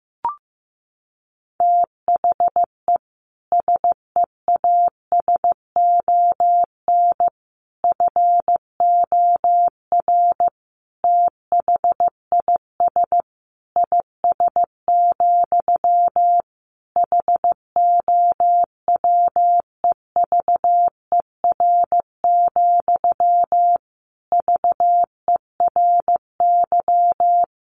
Morse code, heard as a keyed tone, about 15 wpm, 700 Hz. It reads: THE SEASON FOR THIS IS, HOWEVER, VERY
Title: Sentences from Top 1400 Words 15wpm